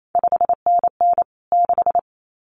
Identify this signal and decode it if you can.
Morse code, heard as a keyed tone, about 28 wpm, 700 Hz. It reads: 5NN 6
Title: CQ WW DX Contest Exchange Encore - 28wpm